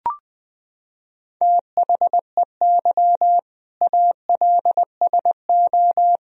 Morse code, heard as a keyed tone, about 21 wpm, 700 Hz. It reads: THEY ALSO
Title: Sentences from Top 1000 Words - Review Part 2 - 20wpm